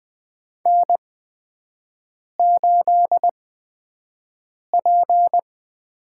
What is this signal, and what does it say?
Morse code, heard as a keyed tone, about 20 wpm, spaced as for 5 wpm, 700 Hz. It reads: N8P
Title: USA Call Signs - 4 Characters - 8x Character Spacing - 20wpm